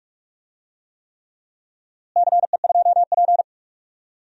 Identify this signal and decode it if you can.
Morse code, heard as a keyed tone, about 41 wpm, 700 Hz. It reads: CE2P
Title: DXCC-100 Call Signs - 4 Characters - Repeated 3x - 45wpm